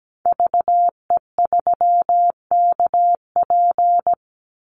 Morse code, heard as a keyed tone, about 17 wpm, 700 Hz. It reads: VE3KP
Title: Call Signs Repeated 3x - 17wpm